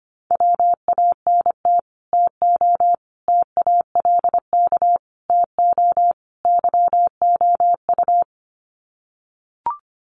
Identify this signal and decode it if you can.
Morse code, heard as a keyed tone, about 25 wpm, 700 Hz. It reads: WANT TO TALK TO YOU
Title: Sets of 5 Words Encore 25wpm